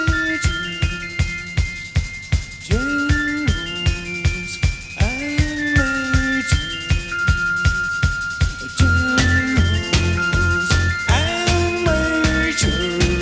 Dial-up modem handshake.